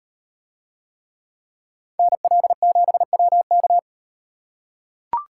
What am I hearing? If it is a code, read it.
Morse code, heard as a keyed tone, about 38 wpm, 700 Hz. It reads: NL7WK